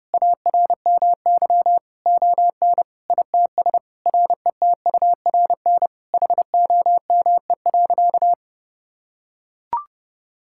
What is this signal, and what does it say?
Morse code, heard as a keyed tone, about 32 wpm, 700 Hz. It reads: ARMY ON ITS RETURN HOME.
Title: Sentences from Top 1000 Words - Review Part 5 - 30wpm